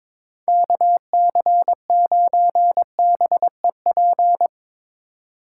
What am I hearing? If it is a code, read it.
Morse code, heard as a keyed tone, about 26 wpm, 700 Hz. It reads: KC9BEP
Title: Speed Racing - USA Call Signs 22wpm